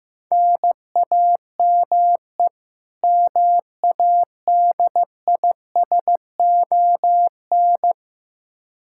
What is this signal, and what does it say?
Morse code, heard as a keyed tone, about 15 wpm, 700 Hz. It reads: NAME MADISON
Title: Practice QSO - Exchange 1 - Beginning - 15wpm